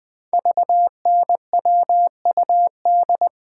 Morse code, heard as a keyed tone, about 20 wpm, 700 Hz. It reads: VNWUD
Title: Code Groups - 5 Letters - 20wpm